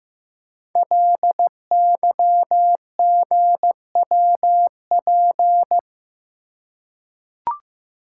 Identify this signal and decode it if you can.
Morse code, heard as a keyed tone, about 15 wpm, 700 Hz. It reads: LYGWP
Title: Code Groups - 5 Letter:Numbers - 15wpm